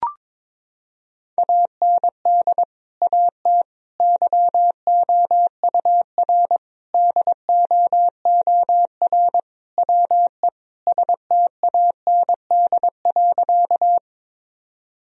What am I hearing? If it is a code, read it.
Morse code, heard as a keyed tone, about 22 wpm, 700 Hz. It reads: AND AT YOUR DOOR WE STAND.
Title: Sentences from Top 1000 Words - Review Part 1 - 22wpm